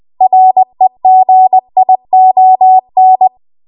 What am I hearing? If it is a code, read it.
Morse code, heard as a keyed tone, about 20 wpm, 750 Hz. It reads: REGION
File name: Archive / 20 WPM / 20WPM